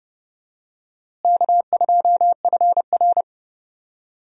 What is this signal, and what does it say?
Morse code, heard as a keyed tone, about 30 wpm, 700 Hz. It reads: K2FR
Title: USA Call Signs - 4 Characters - Repeated 2x - 30wpm